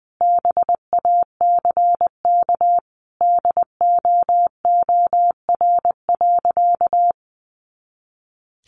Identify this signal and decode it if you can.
Morse code, heard as a keyed tone, about 20 wpm, 700 Hz. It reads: BACK DOOR.
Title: Sentences from Top 300 Words 20wpm